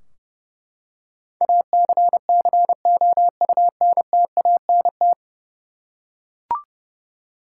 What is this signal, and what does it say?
Morse code, heard as a keyed tone, about 32 wpm, 700 Hz. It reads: ACCOUNTANT
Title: Practice QSO - Exchange 3 - More Info - 30wpm